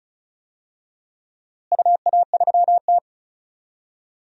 Morse code, heard as a keyed tone, about 34 wpm, 700 Hz. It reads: UA3T
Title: DXCC-100 Call Signs - 4 Characters - Repeated 3x - 35wpm